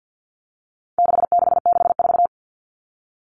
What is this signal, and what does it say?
Morse code, heard as a keyed tone, about 49 wpm, 700 Hz. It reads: 6664
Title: Numbers - 4-Digit 50wpm